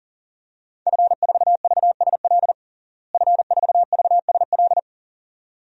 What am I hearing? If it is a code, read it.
Morse code, heard as a keyed tone, about 40 wpm, 700 Hz. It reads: F4VSL F4VSL
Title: DXCC-100 Call Signs - 5 Characters - Repeated 2x - 40wpm